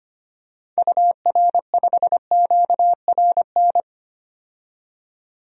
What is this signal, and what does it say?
Morse code, heard as a keyed tone, about 25 wpm, 700 Hz. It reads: UR5QRN
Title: DXCC-100 Call Signs - 6 Characters - 25wpm